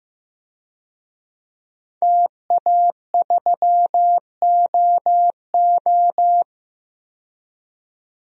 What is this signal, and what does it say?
Morse code, heard as a keyed tone, about 15 wpm, 700 Hz. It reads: TA3OO
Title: DXCC-100 Call Signs - 5 Characters - 15wpm